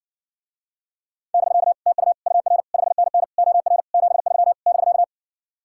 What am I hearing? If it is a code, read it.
Morse code, heard as a keyed tone, about 60 wpm, 700 Hz. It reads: <BK> TU UR 5NN OR 73 <BK>